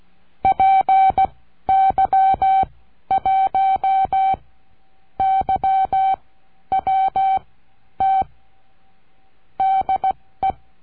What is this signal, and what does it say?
Morse code, heard as a keyed tone, about 16 wpm, spaced as for 10 wpm, 770 Hz. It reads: PY1YWT DE
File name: CURSO DE TELEGRAFIA